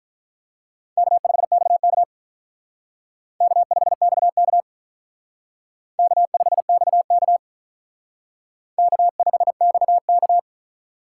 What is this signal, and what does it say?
Morse code, heard as a keyed tone, about 43 wpm, 700 Hz. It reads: K5XK K5XK K5XK K5XK
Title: Speed Racing - USA 1x2 Call Signs 35wpm